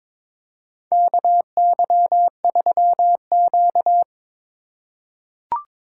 Morse code, heard as a keyed tone, about 22 wpm, 700 Hz. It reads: KY3Q
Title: USA Call Signs - 4 Characters - 3x Character Spacing - 22wpm